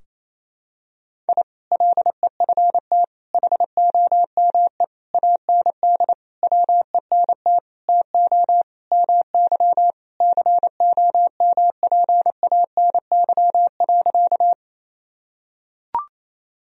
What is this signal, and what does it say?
Morse code, heard as a keyed tone, about 27 wpm, 700 Hz. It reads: I LEFT HOME AND WENT TO MY COMPANY.
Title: Sentences from Top 1000 Words - Review Part 3 - 28wpm